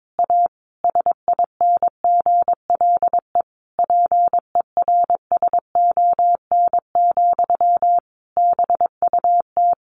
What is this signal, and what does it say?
Morse code, heard as a keyed tone, about 22 wpm, 700 Hz. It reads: A SINGLE PERSON, BUT
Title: Sentences from Top 1400 Words 22wpm